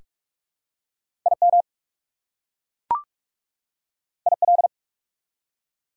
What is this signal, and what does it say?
Morse code, heard as a keyed tone, about 42 wpm, 690 Hz. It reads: IM IL
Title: Prefixes and Suffixes 45wpm